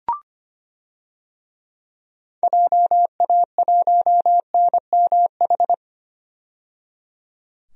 Morse code, heard as a keyed tone, about 25 wpm, 700 Hz. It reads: JA1NMH
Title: DXCC-100 Call Signs - 6 Characters - 25wpm